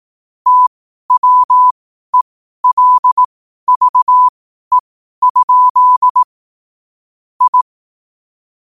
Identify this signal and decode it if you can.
Morse code, heard as a keyed tone, about 18 wpm, spaced as for 12 wpm, 1000 Hz. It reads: TWELVE? I
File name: qotdfiles / 12wpm